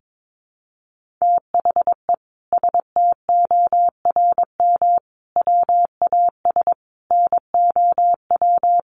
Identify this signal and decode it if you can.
Morse code, heard as a keyed tone, about 22 wpm, 700 Hz. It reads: THE STORM WAS NOW
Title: Sentences from Top 1500 Words 22wpm